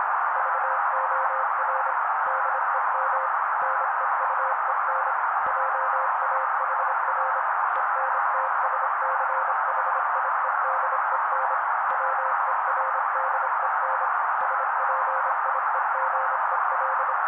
Morse code, heard as a keyed tone, about 25 wpm, 530 Hz. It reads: VOR DEM NEUEN JAHR ENTSCHIEDEN WERDEN SPIEGEL
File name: daily / 25wpm noise